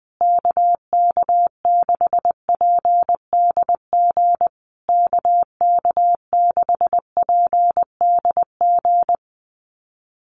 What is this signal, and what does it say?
Morse code, heard as a keyed tone, about 20 wpm, 700 Hz. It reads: KK6PDG KK6PDG
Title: USA Call Signs Repeated 3x 20wpm